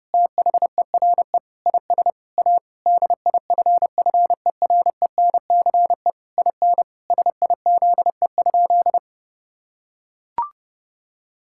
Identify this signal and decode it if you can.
Morse code, heard as a keyed tone, about 30 wpm, 700 Hz. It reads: THERE IS A DIFFERENCE IN SIZE?
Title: Sentences from Top 1000 Words - Review Part 5 - 30wpm